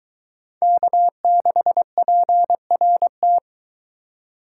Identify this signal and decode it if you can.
Morse code, heard as a keyed tone, about 22 wpm, 700 Hz. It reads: K6PRT